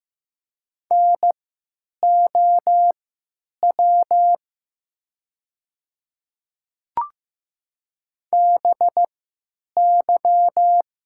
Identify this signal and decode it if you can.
Morse code, heard as a keyed tone, about 15 wpm, spaced as for 9 wpm, 690 Hz. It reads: NOW BY
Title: 3x Character Spacing - Top 100 Words - 15wpm